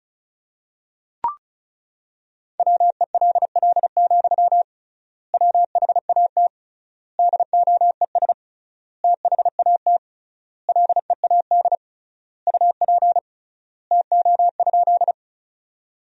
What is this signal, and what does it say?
Morse code, heard as a keyed tone, about 36 wpm, 700 Hz. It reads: WELL, WHAT DOES THAT LEAD UP TO?
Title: 3x Word-Spacing - Sentences from Top 500 Words - 35wpm